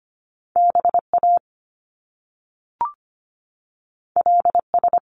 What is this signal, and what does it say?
Morse code, heard as a keyed tone, about 25 wpm, 700 Hz. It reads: BA LS
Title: Top Two Letter Combinations in Words 25wpm